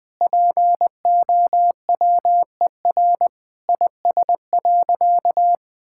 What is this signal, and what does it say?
Morse code, heard as a keyed tone, about 19 wpm, 700 Hz. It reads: POWER IS.
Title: Sentences from Top 1000 Words - Review Part 6 - 20wpm